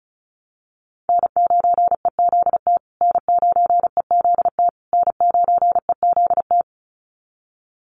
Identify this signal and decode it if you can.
Morse code, heard as a keyed tone, about 34 wpm, 700 Hz. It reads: N9EZT N9EZT N9EZT
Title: USA Call Signs Repeated 3x 35wpm